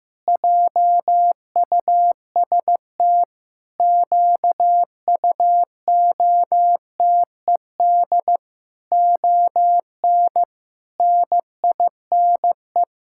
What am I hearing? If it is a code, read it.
Morse code, heard as a keyed tone, about 15 wpm, 700 Hz. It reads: JUST QUOTED ON NINE
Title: Pangram Challenge v2 - 15wpm